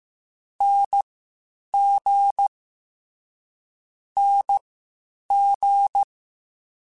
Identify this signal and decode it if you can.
Morse code, heard as a keyed tone, about 15 wpm, spaced as for 8 wpm, 800 Hz. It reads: NG NG
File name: Short Runs / Short Runs-mp3s / Unballanced @13